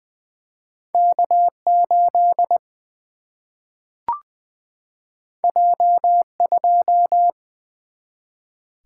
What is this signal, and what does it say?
Morse code, heard as a keyed tone, about 20 wpm, 700 Hz. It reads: K8 J2
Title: Single-Letter Single-Number 20wpm